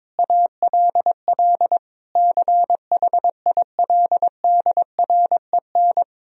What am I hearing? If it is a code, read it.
Morse code, heard as a keyed tone, about 22 wpm, 700 Hz. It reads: ALL CHILDREN
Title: Sentences from Top 1000 Words - Review Part 4 - 22wpm